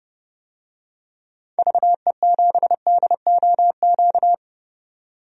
Morse code, heard as a keyed tone, about 30 wpm, 700 Hz. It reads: VE7DOQ
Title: DXCC-100 Call Signs - 6 Characters - 30wpm